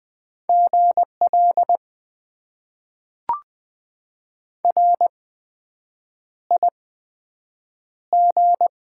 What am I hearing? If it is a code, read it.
Morse code, heard as a keyed tone, about 20 wpm, spaced as for 7 wpm, 700 Hz. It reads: GL RIG
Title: Top 90 QSO Elements - 8x Character Spacing - 20wpm